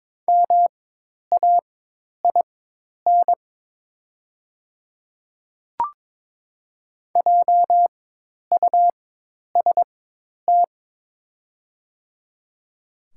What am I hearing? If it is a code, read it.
Morse code, heard as a keyed tone, about 22 wpm, spaced as for 10 wpm, 700 Hz. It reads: MAIN JUST
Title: 4x Character Spacing - Top 500 Words - 22wpm